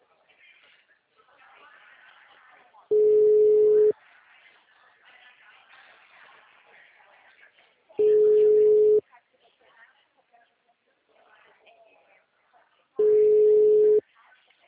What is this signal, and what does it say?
Ringback tone (Europe, 425 Hz), with speech.